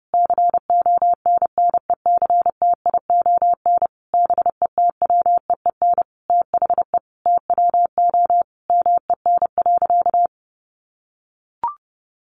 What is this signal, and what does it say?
Morse code, heard as a keyed tone, about 30 wpm, 700 Hz. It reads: CONNECTION BETWEEN THE TWO MEN.
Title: Sentences from Top 1300 Words 30wpm